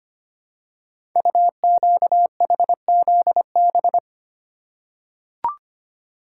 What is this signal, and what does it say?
Morse code, heard as a keyed tone, about 25 wpm, 700 Hz. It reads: UQHZB
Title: Code Groups - 5 Letters - 25wpm